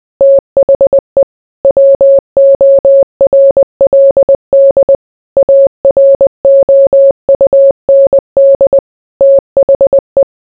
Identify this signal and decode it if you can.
Morse code, heard as a keyed tone, about 20 wpm, 550 Hz. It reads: THE WORLD AROUND THE